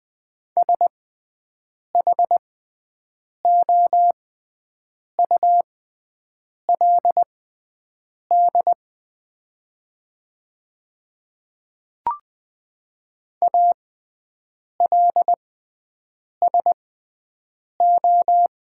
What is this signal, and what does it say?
Morse code, heard as a keyed tone, about 20 wpm, spaced as for 7 wpm, 700 Hz. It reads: SHOULD ALSO
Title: 6x Character Spacing - Top 300 Words - 20wpm